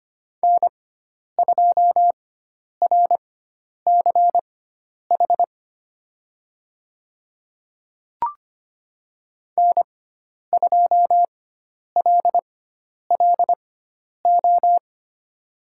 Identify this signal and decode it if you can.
Morse code, heard as a keyed tone, about 25 wpm, spaced as for 10 wpm, 700 Hz. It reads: N2RCH N2LLO
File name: course / 25:10wpm / 095 Lesson 16g - Letter c - Call Signs 25wpm